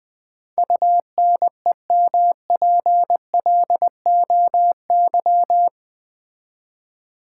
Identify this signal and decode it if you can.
Morse code, heard as a keyed tone, about 20 wpm, 700 Hz. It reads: UNEMPLOY
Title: Build A Word - 4 Letters At A Time - 12-Letter Words - 20wpm